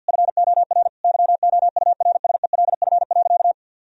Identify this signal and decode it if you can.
Morse code, heard as a keyed tone, about 50 wpm, 700 Hz. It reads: FOR YOURSELF.
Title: Sentences from Top 1000 Words - Review Part 4 - 50wpm